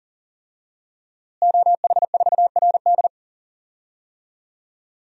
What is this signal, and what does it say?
Morse code, heard as a keyed tone, about 40 wpm, 700 Hz. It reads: OH4RD
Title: DXCC-100 Call Signs - 5 Characters - Repeated 3x - 40wpm